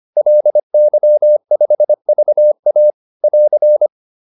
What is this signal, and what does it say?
Morse code, heard as a keyed tone, about 25 wpm, 600 Hz. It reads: LY5VA <AR>